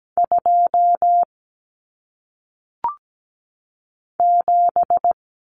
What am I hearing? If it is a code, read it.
Morse code, heard as a keyed tone, about 17 wpm, 700 Hz. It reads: 2 7
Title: Numbers - 1-Digit 17wpm